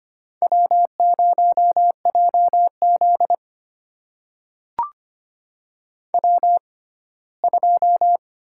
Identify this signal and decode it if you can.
Morse code, heard as a keyed tone, about 25 wpm, spaced as for 17 wpm, 700 Hz. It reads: W0JZ W2
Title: USA Call Signs - 4 Characters - 6x Character Spacing - 25wpm